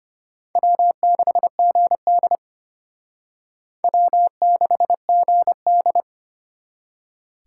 Morse code, heard as a keyed tone, about 32 wpm, 700 Hz. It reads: W6GD W6GD